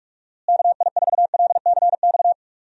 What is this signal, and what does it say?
Morse code, heard as a keyed tone, about 46 wpm, 700 Hz. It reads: KI4LCX
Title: USA Call Signs - 6 Characters - Repeated 2x - 45wpm